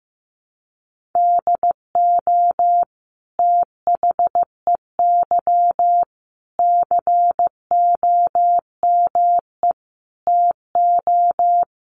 Morse code, heard as a keyed tone, about 15 wpm, 700 Hz. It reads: DO THEY COME TO
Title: Sentences from Top 1500 Words 15wpm